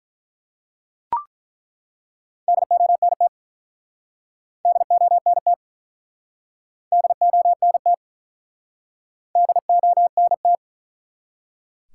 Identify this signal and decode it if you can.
Morse code, heard as a keyed tone, about 47 wpm, 700 Hz. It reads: DONT DONT DONT DONT